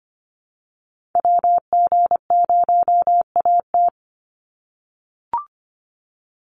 Morse code, heard as a keyed tone, about 25 wpm, 700 Hz. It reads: WG0AT
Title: NA SOTA Activator Callsigns 25wpm